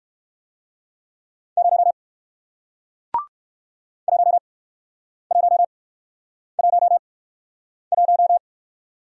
Morse code, heard as a keyed tone, about 61 wpm, spaced as for 9 wpm, 700 Hz. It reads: 0 1111